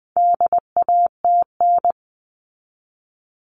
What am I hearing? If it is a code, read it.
Morse code, heard as a keyed tone, about 20 wpm, 700 Hz. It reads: DATN
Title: All Cut Numbers - 4-Digit 20wpm